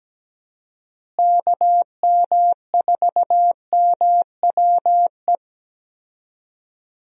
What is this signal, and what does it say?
Morse code, heard as a keyed tone, about 17 wpm, 700 Hz. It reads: KM4MWE
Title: USA Call Signs - 6 Characters - Repeated 3x - 17wpm